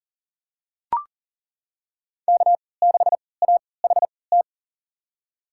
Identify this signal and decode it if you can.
Morse code, heard as a keyed tone, about 40 wpm, spaced as for 23 wpm, 700 Hz. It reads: K6AHT
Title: USA Call Signs - 3x Character Spacing - 40wpm